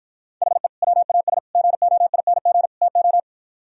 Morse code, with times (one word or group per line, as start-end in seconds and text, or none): 0.4-0.7 HE
0.8-1.4 WAS
1.6-2.7 GOING
2.8-3.2 TO